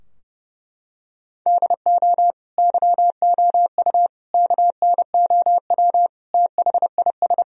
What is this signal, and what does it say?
Morse code, heard as a keyed tone, about 32 wpm, 700 Hz. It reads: DO YOU KNOW THIS
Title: Sentences from Top 1000 Words - Review Part 2 - 30wpm